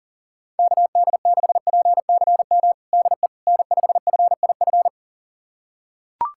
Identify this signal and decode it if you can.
Morse code, heard as a keyed tone, about 40 wpm, 700 Hz. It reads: KD6PCM DE N5FIF
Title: USA Callsign DE Callsign 40wpm